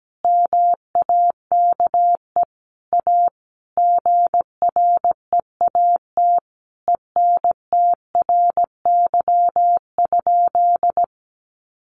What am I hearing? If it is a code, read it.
Morse code, heard as a keyed tone, about 17 wpm, 700 Hz. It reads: MAKE A GREAT ENTRY?
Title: Sentences from Top 1500 Words 17wpm